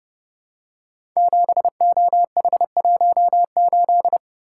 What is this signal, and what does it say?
Morse code, heard as a keyed tone, about 30 wpm, 700 Hz. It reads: 7OH18
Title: Code Groups - 5 Letter:Numbers - 30wpm